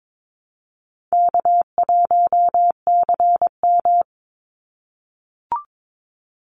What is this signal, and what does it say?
Morse code, heard as a keyed tone, about 22 wpm, 700 Hz. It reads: K1CM
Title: NA SOTA Chaser Callsigns 22wpm